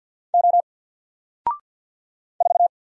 Morse code, heard as a keyed tone, about 48 wpm, 700 Hz. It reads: O 4
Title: Single Character (Letter:Number:Punctuation:Prosign) 50wpm